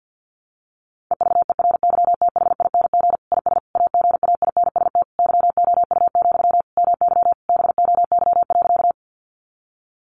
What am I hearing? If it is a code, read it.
Morse code, heard as a keyed tone, about 48 wpm, 700 Hz. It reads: EVERYTHING IS AGAINST YOU, MY BOY.
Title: Sentences from Top 700 Words 50wpm